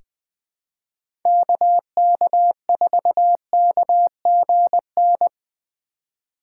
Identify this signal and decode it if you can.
Morse code, heard as a keyed tone, about 20 wpm, 700 Hz. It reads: KK4KGN
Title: USA Call Signs - 6 Characters - 4x Character Spacing - 20wpm